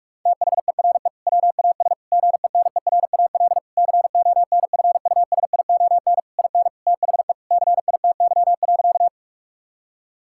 Morse code, heard as a keyed tone, about 46 wpm, 700 Hz. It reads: THERE WAS GENERAL CONFUSION IN THE CITY.
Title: Sentences from Top 1600 Words 45wpm